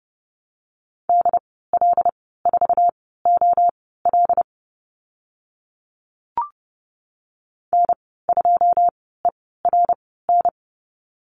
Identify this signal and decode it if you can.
Morse code, heard as a keyed tone, about 30 wpm, spaced as for 17 wpm, 700 Hz. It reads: DL4OL N2ERN
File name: Lesson 14g - Number 2 - Call Signs 30wpm